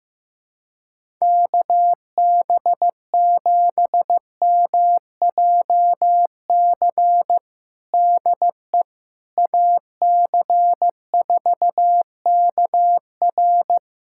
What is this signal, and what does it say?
Morse code, heard as a keyed tone, about 15 wpm, 700 Hz. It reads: KB7MJC DE AC4KR
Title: USA Callsign DE Callsign 15wpm